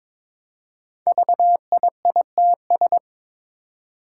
Morse code, heard as a keyed tone, about 22 wpm, 700 Hz. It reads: VIITS
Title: Five Challenging Letter-Number Combinations 22wpm